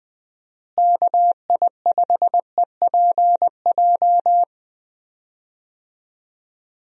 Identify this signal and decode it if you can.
Morse code, heard as a keyed tone, about 20 wpm, 700 Hz. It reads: KI5EPJ
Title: USA Call Signs - 6 Characters - Repeated 2x - 20wpm